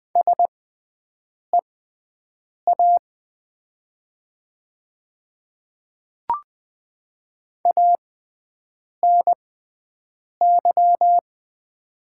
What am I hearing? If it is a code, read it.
Morse code, heard as a keyed tone, about 20 wpm, spaced as for 7 wpm, 690 Hz. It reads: SEA ANY
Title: 6x Character Spacing - Top 500 Words - 20wpm